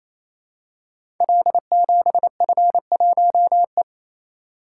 Morse code, heard as a keyed tone, about 29 wpm, 700 Hz. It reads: L7F1E
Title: Code Groups - 5 Letter:Numbers - 28wpm